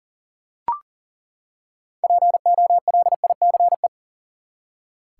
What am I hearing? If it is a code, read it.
Morse code, heard as a keyed tone, about 40 wpm, 700 Hz. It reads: POLICE
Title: Six Letter Words - 40wpm